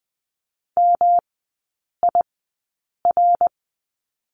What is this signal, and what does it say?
Morse code, heard as a keyed tone, about 20 wpm, spaced as for 8 wpm, 700 Hz. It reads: MIR
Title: 107 Lesson 18e - Letter m - Three Characters 20wpm